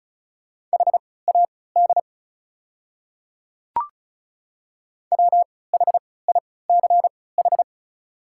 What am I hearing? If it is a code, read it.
Morse code, heard as a keyed tone, about 36 wpm, spaced as for 20 wpm, 690 Hz. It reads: HAD WHICH
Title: 3x Character Spacing - Top 100 Words - 35wpm